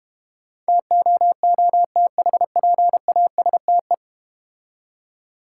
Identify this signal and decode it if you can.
Morse code, heard as a keyed tone, about 32 wpm, 700 Hz. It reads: TOOTHPASTE